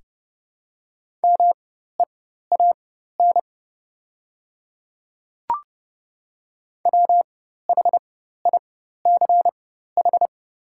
Morse code, heard as a keyed tone, about 30 wpm, spaced as for 14 wpm, 700 Hz. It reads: MEAN WHICH
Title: 4x Character Spacing - Top 500 Words - 30wpm